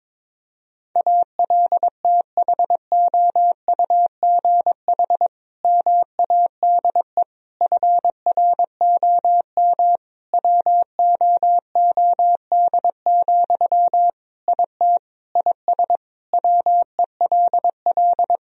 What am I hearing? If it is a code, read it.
Morse code, heard as a keyed tone, about 22 wpm, 700 Hz. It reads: ALTHOUGH MADE FROM WOOD, IT IS WELL
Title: Sentences from Top 1000 Words - Review Part 3 - 22wpm